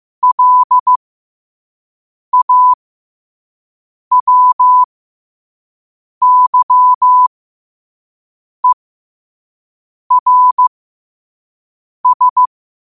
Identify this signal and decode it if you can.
Morse code, heard as a keyed tone, about 15 wpm, spaced as for 5 wpm, 1000 Hz. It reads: LAWYERS
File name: qotdfiles / 5wpm